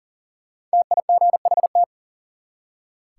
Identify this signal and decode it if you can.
Morse code, heard as a keyed tone, about 40 wpm, 700 Hz. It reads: TIGHT
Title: Five Letter Words - 40wpm